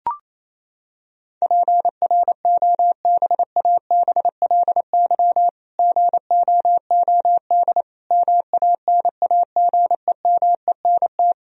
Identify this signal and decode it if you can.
Morse code, heard as a keyed tone, about 28 wpm, 700 Hz. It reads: PROBABLY GOOD MANAGEMENT
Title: Sentences from Top 1000 Words - Review Part 4 - 28wpm